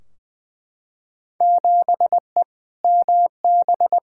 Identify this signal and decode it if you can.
Morse code, heard as a keyed tone, about 21 wpm, 700 Hz. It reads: 7E MB
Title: Field Day Exchanges Encore - 20wpm